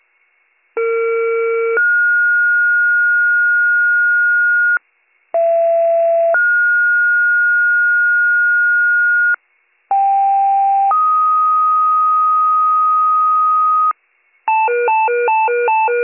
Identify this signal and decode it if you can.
Two-tone page, heard as a one-second tone, then a three-second tone, about 460 then 1430 Hz, about 670 then 1430 Hz, about 790 then 1250 Hz, with noise. It is heard from 0.8 to 13.9 s.